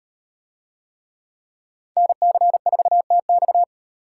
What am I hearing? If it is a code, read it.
Morse code, heard as a keyed tone, about 38 wpm, 700 Hz. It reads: NC4TX